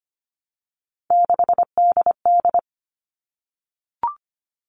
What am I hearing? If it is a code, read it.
Morse code, heard as a keyed tone, about 25 wpm, 700 Hz. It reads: 6DD
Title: All Cut Numbers - 3-Digit 25wpm